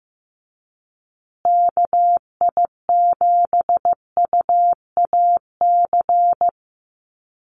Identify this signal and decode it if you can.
Morse code, heard as a keyed tone, about 15 wpm, 700 Hz. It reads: KI7UAC
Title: USA Call Signs Repeated 3x 15wpm